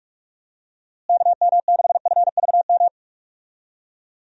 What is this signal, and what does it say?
Morse code, heard as a keyed tone, about 46 wpm, 700 Hz. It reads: KM6FVM
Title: USA Call Signs - 6 Characters - Repeated 2x - 45wpm